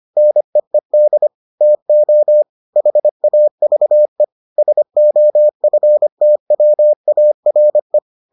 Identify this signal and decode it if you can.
Morse code, heard as a keyed tone, about 25 wpm, 600 Hz. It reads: NEED TO HAVE SOFTWARE